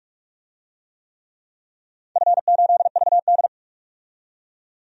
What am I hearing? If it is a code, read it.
Morse code, heard as a keyed tone, about 46 wpm, 700 Hz. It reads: F8VD
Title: DXCC-100 Call Signs - 4 Characters - 45wpm